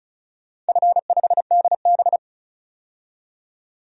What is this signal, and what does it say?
Morse code, heard as a keyed tone, about 36 wpm, 700 Hz. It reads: F5DB
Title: DXCC-100 Call Signs - 4 Characters - 35wpm